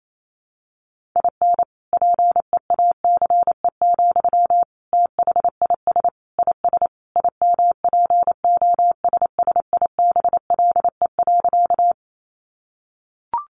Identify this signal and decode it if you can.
Morse code, heard as a keyed tone, about 28 wpm, 700 Hz. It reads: IN PEACE, THIS IS IMPOSSIBLE.
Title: Sentences from Top 1500 Words 28wpm